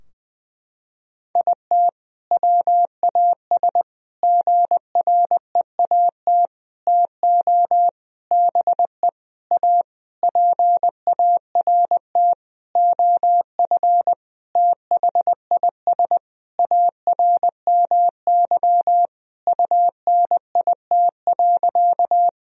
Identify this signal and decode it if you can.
Morse code, heard as a keyed tone, about 20 wpm, 700 Hz. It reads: IT WAS GREAT TO BE A PART OF THIS ARMY UNIT.
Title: Sentences from Top 1000 Words - Review Part 2 - 20wpm